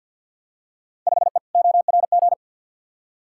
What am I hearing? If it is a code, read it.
Morse code, heard as a keyed tone, about 53 wpm, 700 Hz. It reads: 5E ORG
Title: Field Day Exchanges Encore - 50wpm